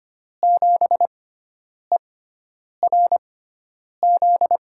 Morse code, heard as a keyed tone, about 25 wpm, spaced as for 9 wpm, 700 Hz. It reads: 7ERZ
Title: USA Call Signs - 5 Characters - 6x Character Spacing - 25wpm